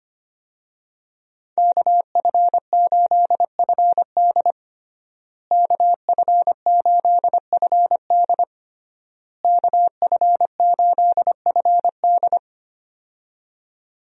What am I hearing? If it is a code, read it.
Morse code, heard as a keyed tone, about 25 wpm, 700 Hz. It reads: KF8FD KF8FD KF8FD
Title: USA Call Signs - 5 Characters - Repeated 3x - 25wpm